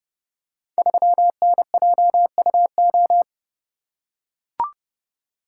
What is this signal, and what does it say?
Morse code, heard as a keyed tone, about 30 wpm, 700 Hz. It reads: 3NJUO
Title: Code Groups - 5 Letter:Numbers - 30wpm